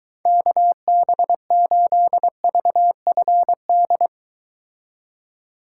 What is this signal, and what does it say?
Morse code, heard as a keyed tone, about 22 wpm, 700 Hz. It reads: KB8VFD